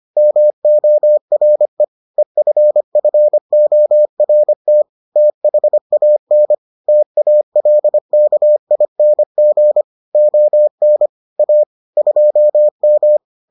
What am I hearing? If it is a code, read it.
Morse code, heard as a keyed tone, about 25 wpm, 600 Hz. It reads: MORE EFFORT THAN TALKING ON A 2M